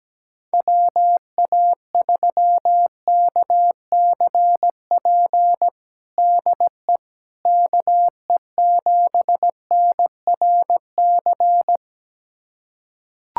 Morse code, heard as a keyed tone, about 17 wpm, 700 Hz. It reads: WA3KCP DE KE7NRC
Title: USA Callsign DE Callsign - 6 Characters - 17wpm